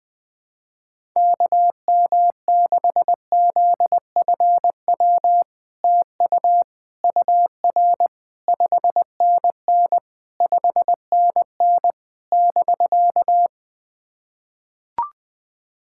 Morse code, heard as a keyed tone, about 20 wpm, 700 Hz. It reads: KM6ZFW TU UR 5NN 5NN <BK>
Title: SOTA QSO Practice 20wpm